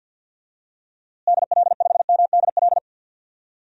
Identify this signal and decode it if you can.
Morse code, heard as a keyed tone, about 53 wpm, 700 Hz. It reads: DL5MDL
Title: DXCC-100 Call Signs - 6 Characters - Repeated 2x - 50wpm